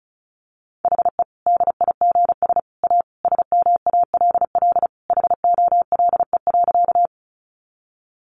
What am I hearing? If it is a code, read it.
Morse code, heard as a keyed tone, about 34 wpm, 700 Hz. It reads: HE DIGS A SMALL HOLE.
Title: Easy Phrases 35wpm